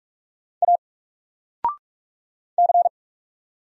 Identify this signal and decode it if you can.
Morse code, heard as a keyed tone, about 44 wpm, 700 Hz. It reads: A C
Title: Single Letter-Number 45wpm